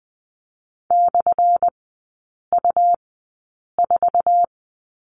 Morse code, heard as a keyed tone, about 20 wpm, spaced as for 8 wpm, 700 Hz. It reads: /U4
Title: 178 Lesson 29e - Slash - Three Characters 20wpm